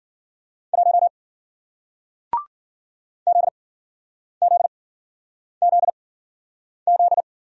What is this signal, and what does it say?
Morse code, heard as a keyed tone, about 52 wpm, spaced as for 9 wpm, 700 Hz. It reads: 1 ZZZZ